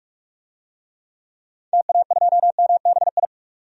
Morse code, heard as a keyed tone, about 46 wpm, 700 Hz. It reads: TA2MBI
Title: DXCC-100 Call Signs - 6 Characters - 45wpm